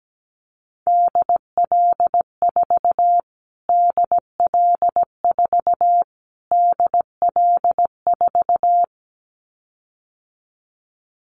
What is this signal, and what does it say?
Morse code, heard as a keyed tone, about 17 wpm, 700 Hz. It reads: DL4 DL4 DL4
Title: Callsign Prefixes for Top 25 DXCC - Repeated 3x - 17wpm